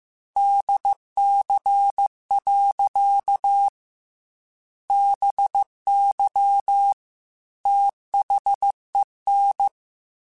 Morse code, heard as a keyed tone, about 15 wpm, 800 Hz. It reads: DC. BY THEN
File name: Technical Subjects / Technical Subjects-mp3s / Sw Pwr Sup @13